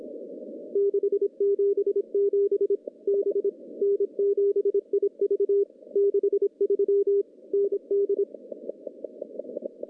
Morse code, heard as a keyed tone, about 25 wpm, 400 Hz. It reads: 677 B N7IV 63 ND
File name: contests / CW-SS-k5tr- -1300